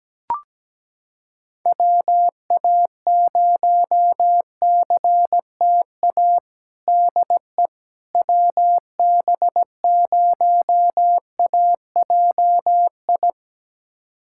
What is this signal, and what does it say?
Morse code, heard as a keyed tone, about 17 wpm, 700 Hz. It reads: WA0CTA DE WB0AJI
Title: USA Callsign DE Callsign - 6 Characters - 17wpm